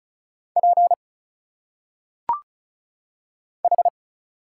Morse code, heard as a keyed tone, about 36 wpm, 700 Hz. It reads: P H
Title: Single Letters - Phonetic Pronunciation 35wpm